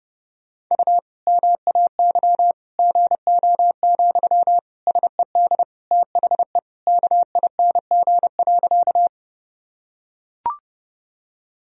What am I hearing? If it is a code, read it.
Morse code, heard as a keyed tone, about 30 wpm, 700 Hz. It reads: U MAY GO, SED THE KING.
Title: Sentences from Top 400 Words and Abbreviations 30wpm